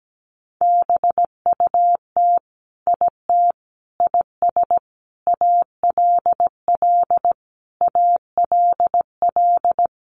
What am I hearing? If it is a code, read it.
Morse code, heard as a keyed tone, about 17 wpm, 700 Hz. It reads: BUT IT IS ALL ALL
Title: Sentences from Top 300 Words 17wpm